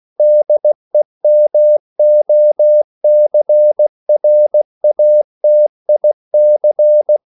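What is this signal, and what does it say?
Morse code, heard as a keyed tone, about 16 wpm, 600 Hz. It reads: DEMOCRATIC
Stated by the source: CW practice of national capitals at 16 words per minute, from ON6ZQ.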